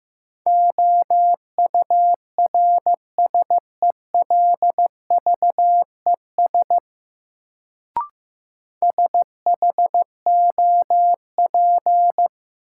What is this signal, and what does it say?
Morse code, heard as a keyed tone, about 15 wpm, 700 Hz. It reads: OURSELVES SHOP
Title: Sentences from Top 1600 Words - New Word Drill Alt - 15wpm